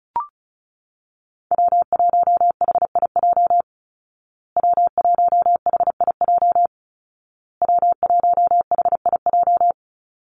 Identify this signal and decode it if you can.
Morse code, heard as a keyed tone, about 34 wpm, 700 Hz. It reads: W1HIJ W1HIJ W1HIJ
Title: Call Signs Repeated 2x - 35wpm